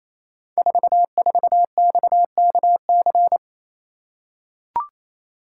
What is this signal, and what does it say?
Morse code, heard as a keyed tone, about 28 wpm, 700 Hz. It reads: 44XKC
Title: Code Groups - 5 Letter:Numbers - 28wpm